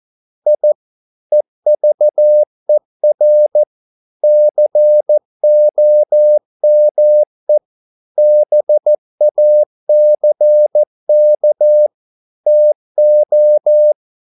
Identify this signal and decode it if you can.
Morse code, heard as a keyed tone, about 14 wpm, 600 Hz. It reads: I EVER COME BACK TO